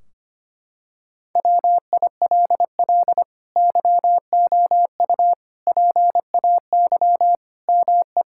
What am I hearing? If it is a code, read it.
Morse code, heard as a keyed tone, about 25 wpm, 700 Hz. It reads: WILL YOU PAY ME
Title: Sentences from Top 1600 Words 25wpm